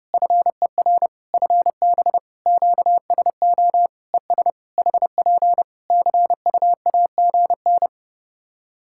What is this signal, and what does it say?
Morse code, heard as a keyed tone, about 30 wpm, 700 Hz. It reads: FER FB QSO ES HP CUAGN
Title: Practice QSO - Exchange 4 - Ending - 30wpm